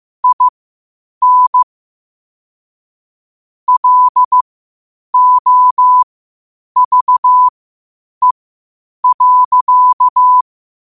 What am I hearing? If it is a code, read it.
Morse code, heard as a keyed tone, about 15 wpm, spaced as for 8 wpm, 1000 Hz. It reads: IN LOVE.
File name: qotdfiles / 8wpm